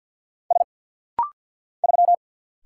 Morse code, heard as a keyed tone, about 51 wpm, 700 Hz. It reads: S 3
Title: Single Letter-Number Alt - Rapid-Fire 50wpm